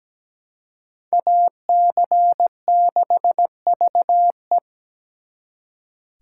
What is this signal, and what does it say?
Morse code, heard as a keyed tone, about 20 wpm, 700 Hz. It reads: AC6VE
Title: Speed Racing - USA 2x2 Call Signs 17wpm